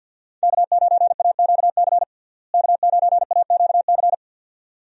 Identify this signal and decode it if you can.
Morse code, heard as a keyed tone, about 48 wpm, 700 Hz. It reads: K9AQC K9AQC
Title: USA Call Signs - 5 Characters - Repeated 2x - 50wpm